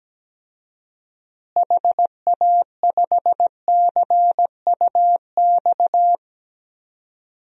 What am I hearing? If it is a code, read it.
Morse code, heard as a keyed tone, about 17 wpm, 700 Hz. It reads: HA5CUX
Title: DXCC-100 Call Signs - 6 Characters - Repeated 2x - 17wpm